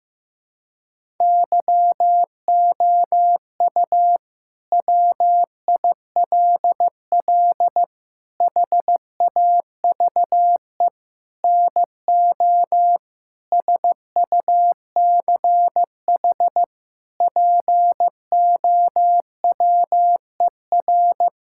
Morse code, heard as a keyed tone, about 15 wpm, 700 Hz. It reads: YOU WILL HAVE NO SUCH POWER
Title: Sentences from Top 1000 Words - Review Part 6 - 15wpm